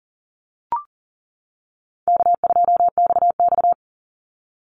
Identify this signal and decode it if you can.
Morse code, heard as a keyed tone, about 41 wpm, 700 Hz. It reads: K2XX
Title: Easy Call Signs - 40wpm